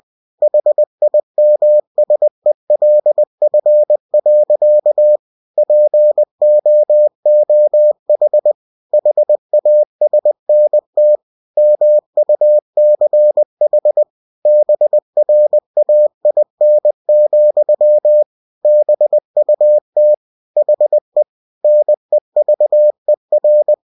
Morse code, heard as a keyed tone, about 20 wpm, 600 Hz. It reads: HIMSELF. POOH HASNT MUCH BRAIN, BUT HE NEVER
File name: cw-books / winnie-the-pooh / winnie-20wpm-page-0091